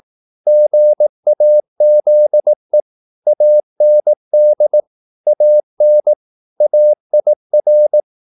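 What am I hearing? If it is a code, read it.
Morse code, heard as a keyed tone, about 18 wpm, 600 Hz. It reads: GAZE AND AN AIR
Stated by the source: Morse practice at 18 wpm, English text, from CW med Gnister.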